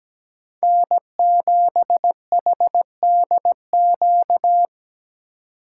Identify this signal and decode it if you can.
Morse code, heard as a keyed tone, about 17 wpm, 700 Hz. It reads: N7HDQ
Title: USA Call Signs - 3x Character Spacing - 17wpm